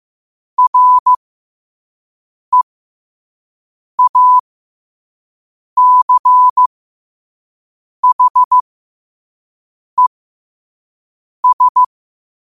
Morse code, heard as a keyed tone, about 15 wpm, spaced as for 5 wpm, 1000 Hz. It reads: REACHES
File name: qotdfiles / 5wpm